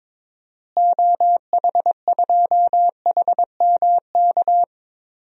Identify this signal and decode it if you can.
Morse code, heard as a keyed tone, about 22 wpm, 700 Hz. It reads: OH2HMK
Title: DXCC-100 Call Signs - 6 Characters - Repeated 3x - 22wpm